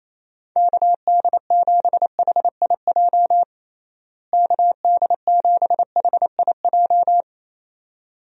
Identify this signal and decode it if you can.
Morse code, heard as a keyed tone, about 28 wpm, 700 Hz. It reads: KD7HIJ KD7HIJ
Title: USA Call Signs - 6 Characters - Repeated 2x - 28wpm